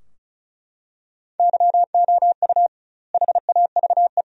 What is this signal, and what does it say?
Morse code, heard as a keyed tone, about 34 wpm, 700 Hz. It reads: YOU HAVE
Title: 2x Word-Spacing - Sentences from Top 100 Words - 35wpm